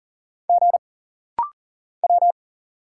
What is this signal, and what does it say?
Morse code, heard as a keyed tone, about 40 wpm, 700 Hz. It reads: G W
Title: Single Letters - Rapid-Fire 40wpm